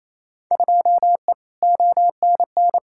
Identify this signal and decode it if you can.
Morse code, heard as a keyed tone, about 28 wpm, 700 Hz. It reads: 2E ONN
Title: Field Day Exchanges Encore - 28wpm